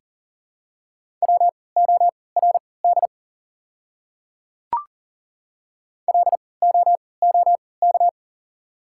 Morse code, heard as a keyed tone, about 39 wpm, spaced as for 22 wpm, 690 Hz. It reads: WORD LOOK
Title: 3x Character Spacing - Top 100 Words - 40wpm